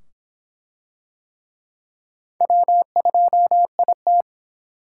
Morse code, heard as a keyed tone, about 24 wpm, 700 Hz. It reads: W2IT